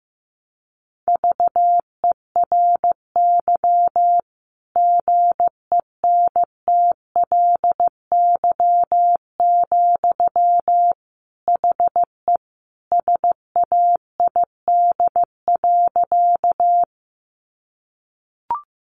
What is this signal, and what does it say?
Morse code, heard as a keyed tone, about 15 wpm, 700 Hz. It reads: VERY GENTLY, HE SAID.
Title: Sentences from Top 1500 Words 15wpm